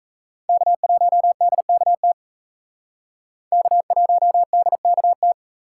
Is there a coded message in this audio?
Yes: Morse code from 0.5 to 5.3 s, heard as a keyed tone, about 36 wpm, 700 Hz. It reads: K1DKT K1DKT